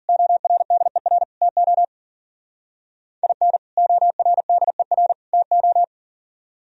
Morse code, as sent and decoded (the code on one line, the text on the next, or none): --- .-. -.. .-. / - --- / .. -. / --- .-. -.. .-. / - ---
ORDER TO IN ORDER TO